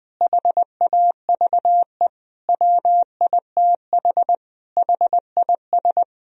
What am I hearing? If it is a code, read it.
Morse code, heard as a keyed tone, about 20 wpm, 700 Hz. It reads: HAVE WITH HIS
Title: Sentences from Top 1000 Words - Review Part 3 - 20wpm